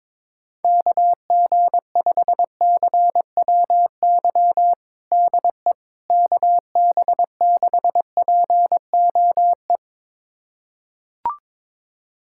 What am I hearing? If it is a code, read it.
Morse code, heard as a keyed tone, about 22 wpm, 700 Hz. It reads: KG5CWY DE KB6POE
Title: USA Callsign DE Callsign - 6 Characters - 22wpm